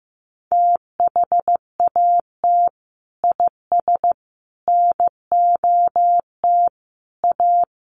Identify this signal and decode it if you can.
Morse code, heard as a keyed tone, about 15 wpm, 700 Hz. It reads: THAT IS NOT A
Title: Sentences from Top 1400 Words 15wpm